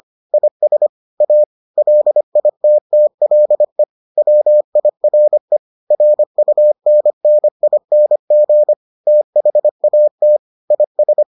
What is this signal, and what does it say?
Morse code, heard as a keyed tone, about 25 wpm, 600 Hz. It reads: IS A LITTLE WIRE RUNNING THAT IS